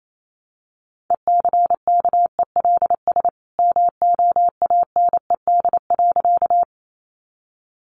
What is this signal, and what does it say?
Morse code, heard as a keyed tone, about 28 wpm, 700 Hz. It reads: ECKELS MOANED.
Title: A Sound Of Thunder 28wpm